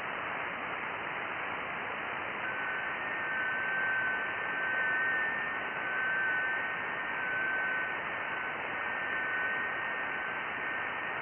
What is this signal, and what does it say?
FT8.
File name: IC-7100